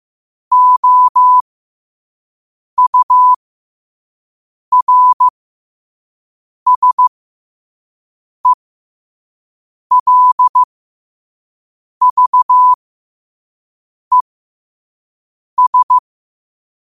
Morse code, heard as a keyed tone, about 15 wpm, spaced as for 5 wpm, 1000 Hz. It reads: OURSELVES
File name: qotdfiles / 5wpm